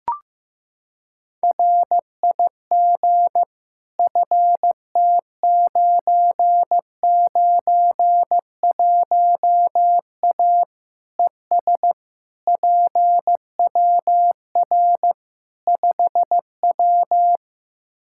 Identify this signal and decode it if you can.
Morse code, heard as a keyed tone, about 15 wpm, 700 Hz. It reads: RIG FT991A ES PWR 5W
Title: Practice QSO - Exchange 2 - Info - 15wpm